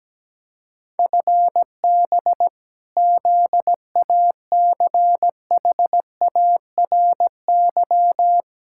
Morse code, heard as a keyed tone, about 17 wpm, 700 Hz. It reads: FB ZACHARY
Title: Practice QSO - Exchange 3 - More Info - 17wpm